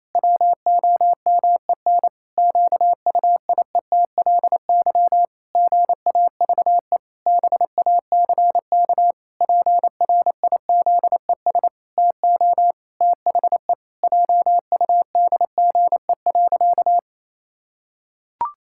Morse code, heard as a keyed tone, about 28 wpm, 700 Hz. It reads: WOMEN QUIETLY GAVE BACK PRIZES TO THE JUDGE.
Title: Pangram Challenge v2 - 28wpm